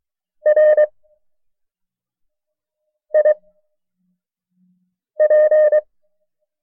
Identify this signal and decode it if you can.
Morse code, heard as a keyed tone, about 23 wpm, 610 Hz. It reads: R I P
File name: FreeFiles / CW-practice / KB6MT-tape1-side2